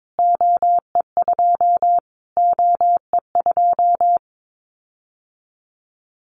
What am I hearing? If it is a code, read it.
Morse code, heard as a keyed tone, about 22 wpm, 700 Hz. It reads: OE2 OE2
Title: Callsign Prefixes for Top 25 DXCC - Repeated 3x - 22wpm